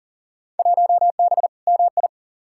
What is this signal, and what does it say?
Morse code, heard as a keyed tone, about 40 wpm, 700 Hz. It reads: 1B MI
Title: Field Day Exchanges - 40wpm